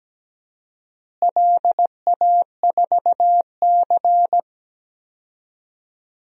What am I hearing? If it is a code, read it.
Morse code, heard as a keyed tone, about 17 wpm, 700 Hz. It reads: LA4C
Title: DXCC-100 Call Signs - 4 Characters - Repeated 2x - 17wpm